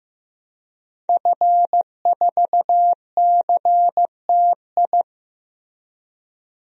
Morse code, heard as a keyed tone, about 15 wpm, 700 Hz. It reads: F4CTI
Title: DXCC-100 Call Signs - 5 Characters - Repeated 3x - 15wpm